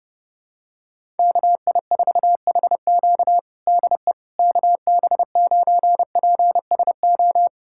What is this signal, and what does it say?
Morse code, heard as a keyed tone, about 32 wpm, 700 Hz. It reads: KI4HQ DE KB9PSO
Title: USA Callsign DE Callsign 30wpm